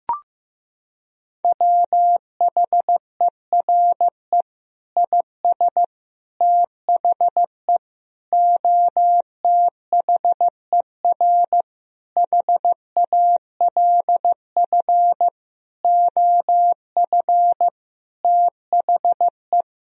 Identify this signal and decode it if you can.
Morse code, heard as a keyed tone, about 16 wpm, 700 Hz. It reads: WHERE IS THE OTHER HALF OF THE
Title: Sentences from Top 1000 Words - Review Part 2 - 15wpm